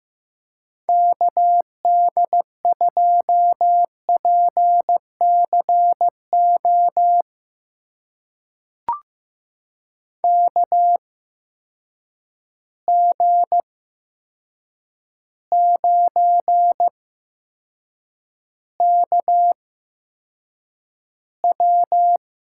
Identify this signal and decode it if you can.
Morse code, heard as a keyed tone, about 15 wpm, spaced as for 7 wpm, 700 Hz. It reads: KD2PCO KG9KW
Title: USA Call Signs - 8x Character Spacing - 15wpm